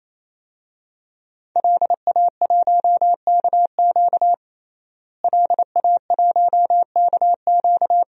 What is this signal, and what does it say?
Morse code, heard as a keyed tone, about 28 wpm, 700 Hz. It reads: LA1KQ LA1KQ
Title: DXCC-100 Call Signs - 5 Characters - Repeated 3x - 28wpm